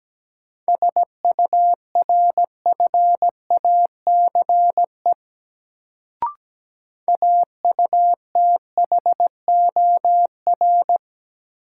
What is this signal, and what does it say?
Morse code, heard as a keyed tone, about 17 wpm, 700 Hz. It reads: SURFACE AUTHOR
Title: Sets of 4 Words - New Word Drill - 17wpm